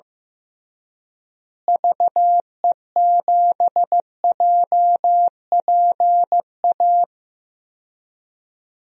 Morse code, heard as a keyed tone, about 16 wpm, 700 Hz. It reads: VE7JPA
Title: DXCC-100 Call Signs - 6 Characters - Repeated 2x - 15wpm